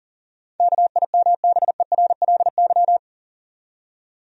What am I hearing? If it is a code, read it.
Morse code, heard as a keyed tone, about 40 wpm, 700 Hz. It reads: KIMBERLY
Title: USA Common Names - 40wpm